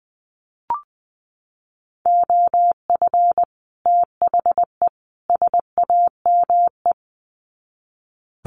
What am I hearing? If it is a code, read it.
Morse code, heard as a keyed tone, about 20 wpm, 700 Hz. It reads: OF THE SAME
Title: Sets of 3 Words 20wpm